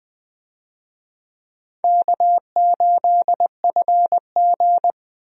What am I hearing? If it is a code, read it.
Morse code, heard as a keyed tone, about 20 wpm, 700 Hz. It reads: K8FG
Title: USA Call Signs - 4 Characters - Repeated 3x - 20wpm